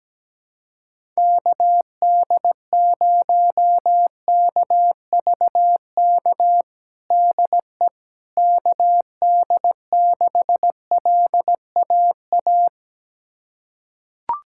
Morse code, heard as a keyed tone, about 17 wpm, 700 Hz. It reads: KD0KVK DE KD6LAA
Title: USA Callsign DE Callsign - 6 Characters - 17wpm